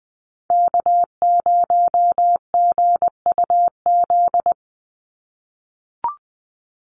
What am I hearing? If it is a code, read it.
Morse code, heard as a keyed tone, about 20 wpm, 700 Hz. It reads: K0GUZ
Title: Call Signs - 20wpm